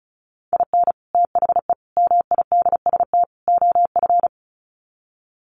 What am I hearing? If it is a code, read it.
Morse code, heard as a keyed tone, about 34 wpm, 700 Hz. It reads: IN THE MIDST OF
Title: Sets of 4 Words 35wpm